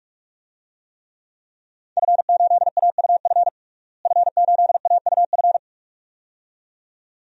Morse code, heard as a keyed tone, about 46 wpm, 700 Hz. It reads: F8AUF F8AUF
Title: DXCC-100 Call Signs - 5 Characters - Repeated 2x - 45wpm